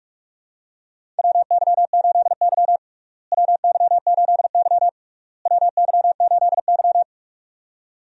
Morse code, heard as a keyed tone, about 41 wpm, 700 Hz. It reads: WY8Y WY8Y WY8Y
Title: USA Call Signs - 4 Characters - Repeated 3x - 45wpm